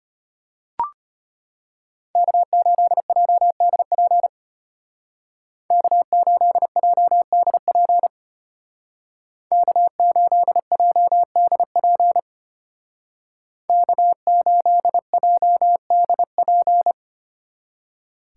Morse code, heard as a keyed tone, about 33 wpm, 700 Hz. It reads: K8JDP K8JDP K8JDP K8JDP